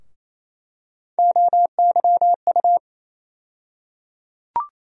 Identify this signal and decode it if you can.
Morse code, heard as a keyed tone, about 28 wpm, 700 Hz. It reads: OYU
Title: USA Call Sign Postfixes - 3 Letters 28wpm